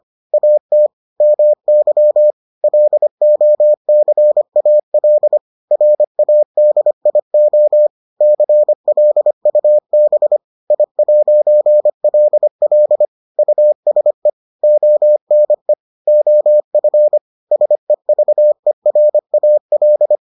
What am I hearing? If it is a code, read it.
Morse code, heard as a keyed tone, about 25 wpm, 600 Hz. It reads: AT MY LOCAL RADIO CLUB I'LL USE ONE OF SEVERAL